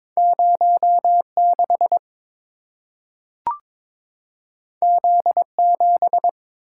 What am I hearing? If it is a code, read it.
Morse code, heard as a keyed tone, about 23 wpm, 700 Hz. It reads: 06 Z7
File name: practice / 22wpm / Two Letter-Number Combinations - 22wpm